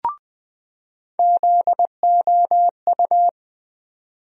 Morse code, heard as a keyed tone, about 20 wpm, 700 Hz. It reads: ZOU
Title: USA Call Sign Postfixes - 3 Letters 20wpm